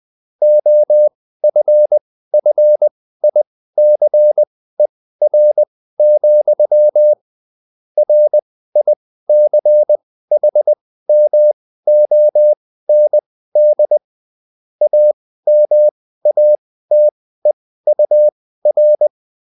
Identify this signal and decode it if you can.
Morse code, heard as a keyed tone, about 20 wpm, spaced as for 14 wpm, 600 Hz. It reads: OFFICER, RICHMOND AMATEUR